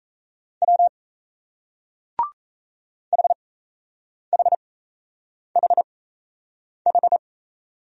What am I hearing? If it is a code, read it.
Morse code, heard as a keyed tone, about 35 wpm, spaced as for 8 wpm, 700 Hz. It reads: W HHHH